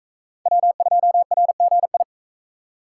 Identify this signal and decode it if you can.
Morse code, heard as a keyed tone, about 37 wpm, 700 Hz. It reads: W2RGI